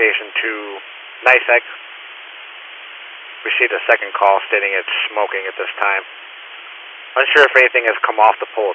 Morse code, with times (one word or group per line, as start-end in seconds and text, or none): none